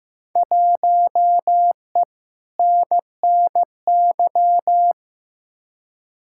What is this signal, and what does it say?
Morse code, heard as a keyed tone, about 15 wpm, 700 Hz. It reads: 1E NNY
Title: Field Day Exchanges - 15wpm